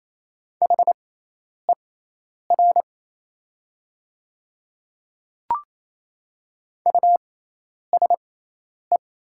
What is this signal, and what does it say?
Morse code, heard as a keyed tone, about 28 wpm, spaced as for 10 wpm, 700 Hz. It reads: HER USE
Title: 6x Character Spacing - Top 100 Words - 28wpm